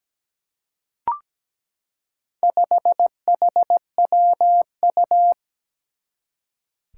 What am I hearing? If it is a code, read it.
Morse code, heard as a keyed tone, about 17 wpm, 700 Hz. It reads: 5HWU
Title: Four Challenging Letter-Number Combinations 17wpm